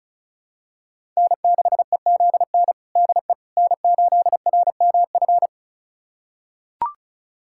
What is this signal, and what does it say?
Morse code, heard as a keyed tone, about 36 wpm, 700 Hz. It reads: N6EZN DE N8RMF
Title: USA Callsign DE Callsign - 5 Characters - 35wpm